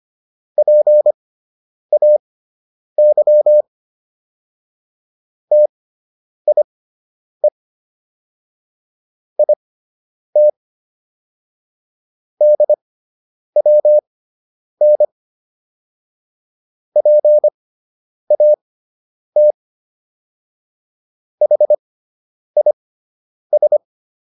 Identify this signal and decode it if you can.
Morse code, heard as a keyed tone, about 25 wpm, spaced as for 9 wpm, 600 Hz. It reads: PAY TIE IT DWN PAT HIS